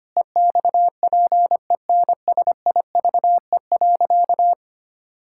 Morse code, heard as a keyed tone, about 25 wpm, 700 Hz. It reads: EXPENSIVE.
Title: Pangram Challenge v2 - 25wpm